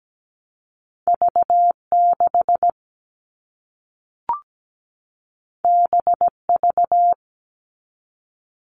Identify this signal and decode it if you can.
Morse code, heard as a keyed tone, about 17 wpm, 700 Hz. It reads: V6 BV
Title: All Cut Numbers - 2-Digit 17wpm